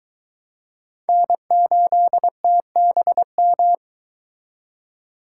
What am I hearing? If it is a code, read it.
Morse code, heard as a keyed tone, about 22 wpm, 700 Hz. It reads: N8TBM